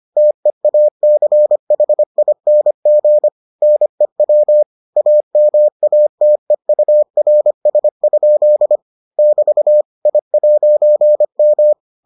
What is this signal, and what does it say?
Morse code, heard as a keyed tone, about 25 wpm, 600 Hz. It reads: TEACHING NEW AMATEURS? <BT> I'M